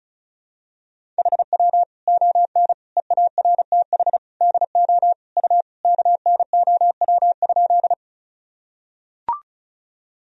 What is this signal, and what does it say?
Morse code, heard as a keyed tone, about 36 wpm, 700 Hz. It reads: HW ON EARTH DO U KNOW?
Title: Sentences from Top 200 Words and Abbreviations 35wpm